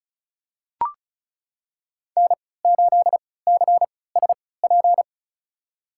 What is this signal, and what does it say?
Morse code, heard as a keyed tone, about 34 wpm, spaced as for 20 wpm, 700 Hz. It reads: N8CSP
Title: USA Call Signs - 3x Character Spacing - 35wpm